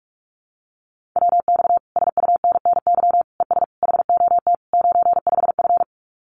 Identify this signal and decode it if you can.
Morse code, heard as a keyed tone, about 44 wpm, 700 Hz. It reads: WX SUNNY ES HOT 95F
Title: Common QSO Phrases Encore 45wpm